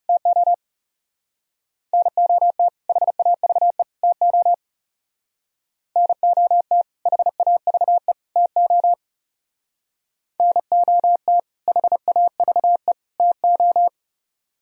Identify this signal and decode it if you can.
Morse code, heard as a keyed tone, about 36 wpm, 700 Hz. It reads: TO NOT HAVE TO NOT HAVE TO NOT HAVE TO
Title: Speed-Racing - Sets of 3 Words 30wpm